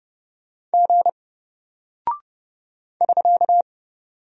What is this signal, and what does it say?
Morse code, heard as a keyed tone, about 30 wpm, 700 Hz. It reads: G <SK>
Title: Single Character (Letter:Number:Punctuation:Prosign) 30wpm